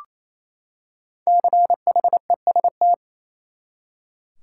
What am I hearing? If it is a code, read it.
Morse code, heard as a keyed tone, about 28 wpm, 700 Hz. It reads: CHEST
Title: Five Letter Words - 28wpm